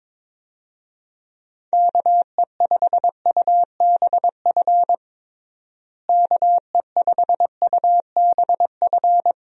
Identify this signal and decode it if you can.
Morse code, heard as a keyed tone, about 22 wpm, 700 Hz. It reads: KE5UBF KE5UBF
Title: USA Call Signs - 6 Characters - Repeated 2x - 22wpm